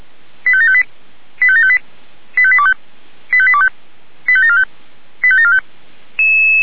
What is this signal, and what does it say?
Five-tone selective call (ZVEI), heard as quick runs of five tones, tones of about 70 ms. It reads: 85758 85758 85825 85825 85745 85745
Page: SDR/ZVEI